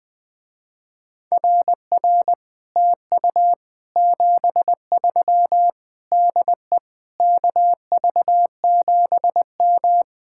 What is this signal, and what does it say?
Morse code, heard as a keyed tone, about 20 wpm, 700 Hz. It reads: RR TU 73 DE KV7M
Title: Speed-Racing Common QSO Phrases Encore - 17wpm